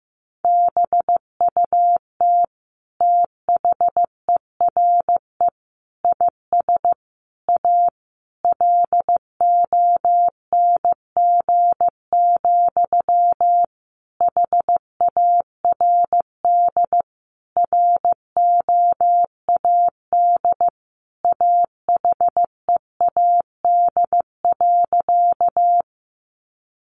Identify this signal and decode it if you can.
Morse code, heard as a keyed tone, about 15 wpm, 700 Hz. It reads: BUT THERE IS A LONG, HARD ROAD AHEAD.
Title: Sentences from Top 1100 Words 15wpm